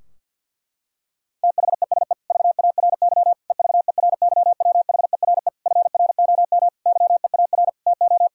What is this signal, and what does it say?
Morse code, heard as a keyed tone, about 50 wpm, 700 Hz. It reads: THESE VARY EVERYWHERE FROM YEAR TO
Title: Sentences from Top 1600 Words 50wpm